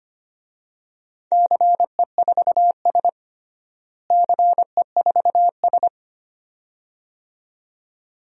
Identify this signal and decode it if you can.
Morse code, heard as a keyed tone, about 25 wpm, 700 Hz. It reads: CE4S CE4S
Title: DXCC-100 Call Signs - 4 Characters - Repeated 2x - 25wpm